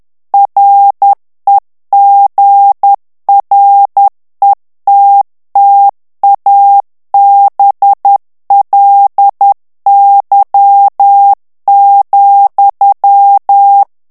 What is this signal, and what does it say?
Morse code, heard as a keyed tone, about 10 wpm, 800 Hz. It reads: REGRETTABLY,
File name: morse mp3 / ANGLAIS texte clair 2 vitesse 10 mots minute